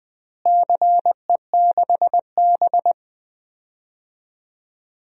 Morse code, heard as a keyed tone, about 20 wpm, 700 Hz. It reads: CE6B
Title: DXCC-100 Call Signs - 4 Characters - Repeated 2x - 20wpm